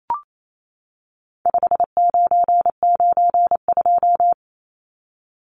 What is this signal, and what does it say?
Morse code, heard as a keyed tone, about 28 wpm, 700 Hz. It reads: 5992
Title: Numbers - 4-Digit 28wpm